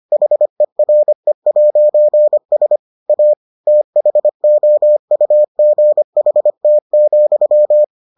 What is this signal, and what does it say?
Morse code, heard as a keyed tone, about 25 wpm, 600 Hz. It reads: HERE'S A THOUGHT,